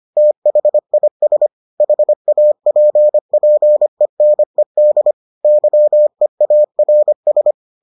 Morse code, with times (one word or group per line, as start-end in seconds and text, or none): 0.2-1.5 THIS
1.8-5.1 HAPPENED
5.4-7.5 YEARS